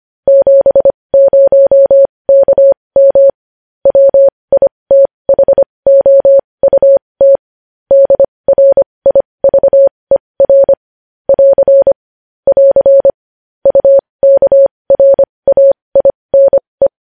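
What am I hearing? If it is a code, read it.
Morse code, heard as a keyed tone, about 25 wpm, spaced as for 20 wpm, 550 Hz. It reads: 70KM WITHOUT DRIVER <AR> <AR> UKRAINE
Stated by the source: Morse code news headlines sent at 20.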